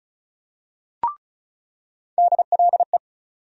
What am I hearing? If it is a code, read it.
Morse code, heard as a keyed tone, about 35 wpm, 700 Hz. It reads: DLE
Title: USA Call Sign Postfixes - 3 Letters 35wpm